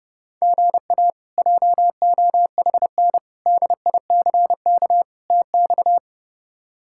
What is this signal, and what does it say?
Morse code, heard as a keyed tone, about 30 wpm, 700 Hz. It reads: GA JOHN DICK TX
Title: K1USN SST QSO Practice - 30wpm